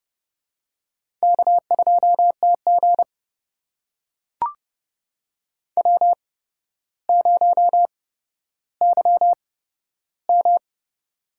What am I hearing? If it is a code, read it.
Morse code, heard as a keyed tone, about 29 wpm, spaced as for 13 wpm, 700 Hz. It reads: K2TG W0YM
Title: USA Call Signs - 4 Characters - 8x Character Spacing - 30wpm